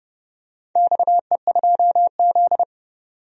Morse code, heard as a keyed tone, about 29 wpm, 700 Hz. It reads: XE2Z
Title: DXCC-100 Call Signs - 4 Characters - 30wpm